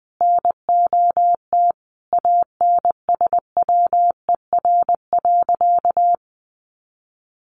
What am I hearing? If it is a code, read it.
Morse code, heard as a keyed tone, about 20 wpm, 700 Hz. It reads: NOT ANSWER.
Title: Sentences from Top 600 Words 20wpm